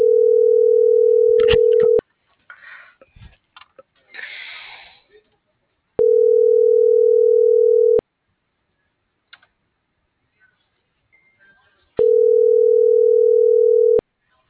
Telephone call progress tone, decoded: ringback tone

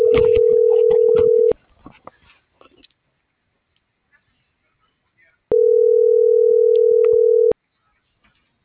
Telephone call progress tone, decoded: ringback tone

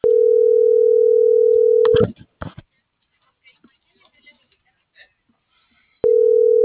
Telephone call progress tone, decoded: ringback tone

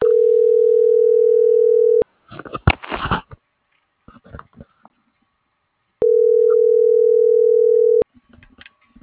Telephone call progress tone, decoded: ringback tone

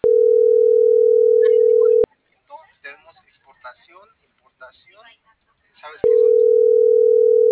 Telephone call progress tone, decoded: ringback tone

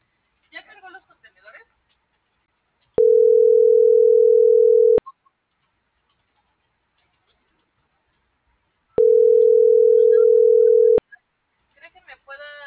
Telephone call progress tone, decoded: ringback tone

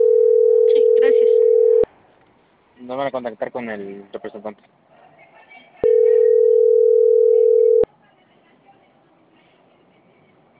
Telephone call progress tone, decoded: ringback tone

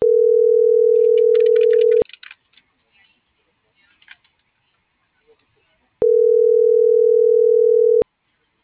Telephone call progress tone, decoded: ringback tone

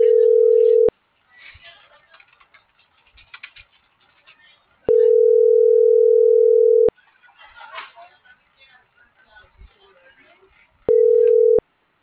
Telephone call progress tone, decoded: ringback tone

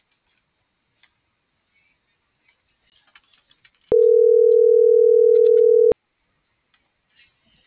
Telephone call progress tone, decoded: ringback tone

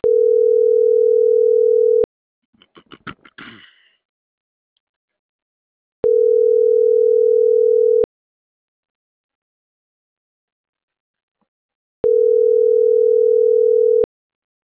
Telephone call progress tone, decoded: ringback tone

